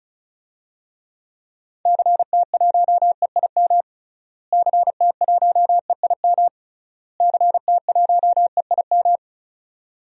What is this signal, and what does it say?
Morse code, heard as a keyed tone, about 36 wpm, 700 Hz. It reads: CT1EIM CT1EIM CT1EIM
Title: DXCC-100 Call Signs - 6 Characters - Repeated 3x - 35wpm